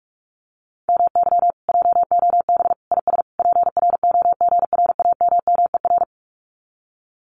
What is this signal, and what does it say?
Morse code, heard as a keyed tone, about 46 wpm, 700 Hz. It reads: MY JOB IS PROGRAMMER
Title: Common QSO Phrases Encore 45wpm